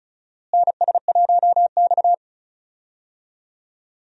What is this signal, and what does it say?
Morse code, heard as a keyed tone, about 36 wpm, 700 Hz. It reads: NS1X
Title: USA Call Signs - 4 Characters - Repeated 2x - 35wpm